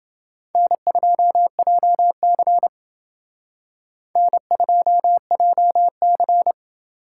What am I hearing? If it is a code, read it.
Morse code, heard as a keyed tone, about 28 wpm, 700 Hz. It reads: N2JC N2JC